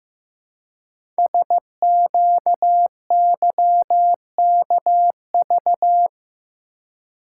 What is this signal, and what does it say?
Morse code, heard as a keyed tone, about 15 wpm, 700 Hz. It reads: SQYKV
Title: Code Groups - 5 Letters - 15wpm